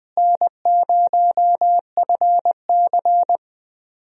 Morse code, heard as a keyed tone, about 20 wpm, 700 Hz. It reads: N0FC
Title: USA Call Signs - 4 Characters - Repeated 2x - 20wpm